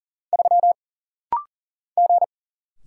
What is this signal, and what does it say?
Morse code, heard as a keyed tone, about 40 wpm, 700 Hz. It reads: 3 G
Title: Single Letter-Number - Rapid-Fire 40wpm